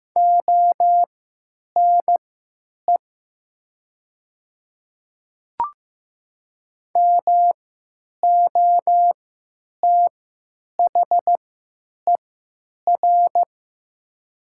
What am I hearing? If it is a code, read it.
Morse code, heard as a keyed tone, about 15 wpm, spaced as for 8 wpm, 700 Hz. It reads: ONE MOTHER
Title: 3x Character Spacing - Top 200 Words - 15wpm